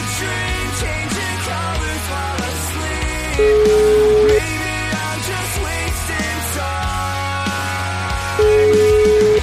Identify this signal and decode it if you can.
Ringback tone (Europe, 425 Hz), with noise.